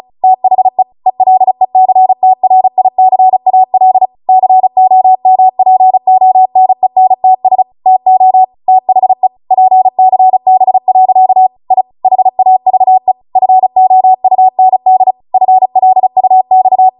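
Morse code, heard as a keyed tone, about 36 wpm, 750 Hz. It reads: THE ELECTRICAL COMPONENTS TO THE PCB. I HAVE FOUND FLUX
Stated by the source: Amateur radio Morse code practice archive transmission at 35 wpm.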